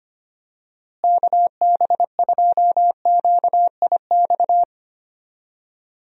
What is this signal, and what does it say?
Morse code, heard as a keyed tone, about 25 wpm, 700 Hz. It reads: KB2QIX
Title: USA Call Signs - 6 Characters - Repeated 3x - 25wpm